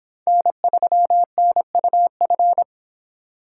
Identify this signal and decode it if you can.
Morse code, heard as a keyed tone, about 26 wpm, 700 Hz. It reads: N3NUF